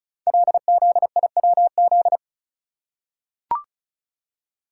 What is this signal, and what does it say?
Morse code, heard as a keyed tone, about 36 wpm, 690 Hz. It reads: LZIWZ
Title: Code Groups - 5 Letters - 35wpm